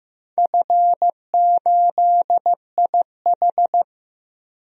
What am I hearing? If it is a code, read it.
Morse code, heard as a keyed tone, about 15 wpm, 700 Hz. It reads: F8IH
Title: DXCC-100 Call Signs - 4 Characters - Repeated 3x - 15wpm